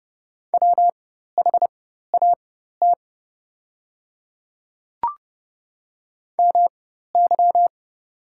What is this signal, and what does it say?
Morse code, heard as a keyed tone, about 30 wpm, spaced as for 14 wpm, 700 Hz. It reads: WHAT MY
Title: 4x Character Spacing - Top 100 Words - 30wpm